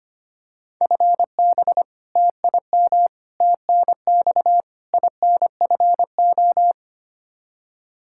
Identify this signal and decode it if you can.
Morse code, heard as a keyed tone, about 25 wpm, 700 Hz. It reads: FB TIM TNX INFO
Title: K1USN SST QSO Practice 25wpm